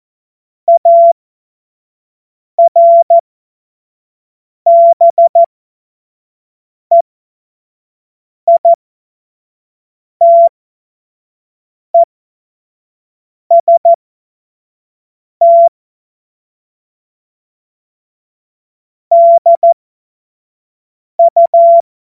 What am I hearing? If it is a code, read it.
Morse code, heard as a keyed tone, about 14 wpm, spaced as for 5 wpm, 680 Hz. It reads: ARBEITEST DU
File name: Morsekurs / Witze / Witz 22. November 2022 25 F70